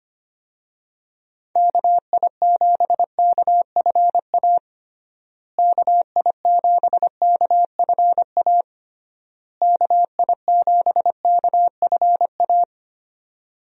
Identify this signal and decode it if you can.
Morse code, heard as a keyed tone, about 25 wpm, 700 Hz. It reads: KI7KFA KI7KFA KI7KFA
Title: USA Call Signs - 6 Characters - Repeated 3x - 25wpm